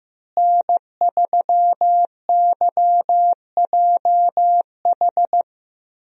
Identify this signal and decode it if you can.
Morse code, heard as a keyed tone, about 15 wpm, 700 Hz. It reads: N3YJH
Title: USA Call Signs - 5 Characters - Repeated 3x - 15wpm